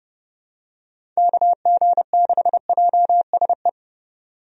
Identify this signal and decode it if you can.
Morse code, heard as a keyed tone, about 30 wpm, 700 Hz. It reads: KG6JSE
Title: USA Call Signs - 6 Characters - 30wpm